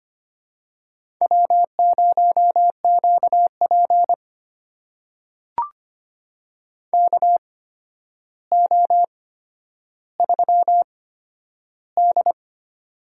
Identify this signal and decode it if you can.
Morse code, heard as a keyed tone, about 25 wpm, spaced as for 11 wpm, 700 Hz. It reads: W0QP KO3D
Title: USA Call Signs - 4 Characters - 8x Character Spacing - 25wpm